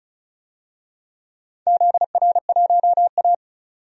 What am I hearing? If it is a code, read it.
Morse code, heard as a keyed tone, about 36 wpm, 700 Hz. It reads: ZR1A
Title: DXCC-100 Call Signs - 4 Characters - 35wpm